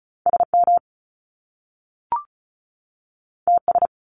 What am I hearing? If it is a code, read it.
Morse code, heard as a keyed tone, about 36 wpm, 700 Hz. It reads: SM TS
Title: Top Two Letter Combinations in Words 35wpm